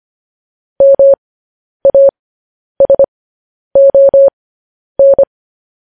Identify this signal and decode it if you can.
Morse code, heard as a keyed tone, about 25 wpm, spaced as for 10 wpm, 550 Hz. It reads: MASON
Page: news headlines in morse code at 10 wpm